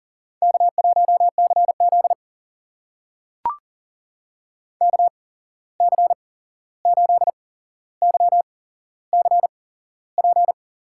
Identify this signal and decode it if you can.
Morse code, heard as a keyed tone, about 41 wpm, spaced as for 15 wpm, 690 Hz. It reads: K1CZ KC8YCP
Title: USA Call Signs - 8x Character Spacing - 40wpm